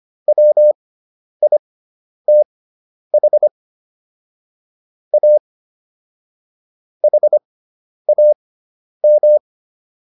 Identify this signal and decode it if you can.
Morse code, heard as a keyed tone, about 25 wpm, spaced as for 10 wpm, 600 Hz. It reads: WITH A HAM